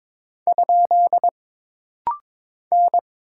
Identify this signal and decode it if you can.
Morse code, heard as a keyed tone, about 22 wpm, 700 Hz. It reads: ? N
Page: This rapid-fire timing is geared to push your ICR (Instant Character Recognition) skills.